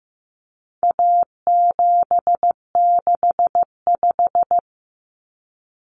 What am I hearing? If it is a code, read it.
Morse code, heard as a keyed tone, about 15 wpm, 700 Hz. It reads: A765
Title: Common Cut Numbers - 4-Digit 15wpm